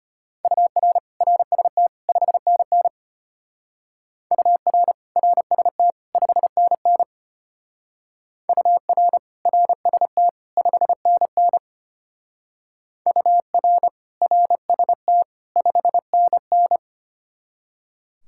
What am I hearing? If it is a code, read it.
Morse code, heard as a keyed tone, about 32 wpm, 700 Hz. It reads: UR RST 5NN UR RST 5NN UR RST 5NN UR RST 5NN